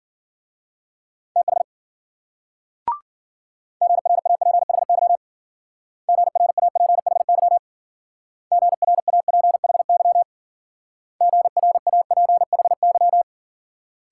Morse code, heard as a keyed tone, about 50 wpm, 700 Hz. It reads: TH GRAPHY GRAPHY GRAPHY GRAPHY